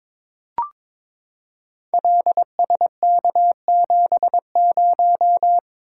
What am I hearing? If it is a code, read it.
Morse code, heard as a keyed tone, about 23 wpm, 700 Hz. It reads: LSK70
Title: Code Groups - 5 Letter:Numbers - 22wpm